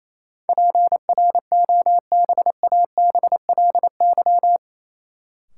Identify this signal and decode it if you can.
Morse code, heard as a keyed tone, about 28 wpm, 700 Hz. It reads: PROBABLY
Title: Sentences from Top 700 Words - New Word Drill - 28wpm